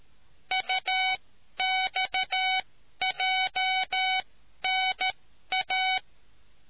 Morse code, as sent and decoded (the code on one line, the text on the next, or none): ..- -..- .--- -. .-
UXJNA